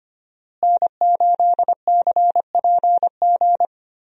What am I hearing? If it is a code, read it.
Morse code, heard as a keyed tone, about 25 wpm, 700 Hz. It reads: N8CPG
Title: USA Call Signs - 5 Characters - Repeated 3x - 25wpm